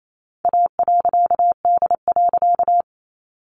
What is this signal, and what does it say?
Morse code, heard as a keyed tone, about 28 wpm, 700 Hz. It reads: A.D.
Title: A Sound Of Thunder 28wpm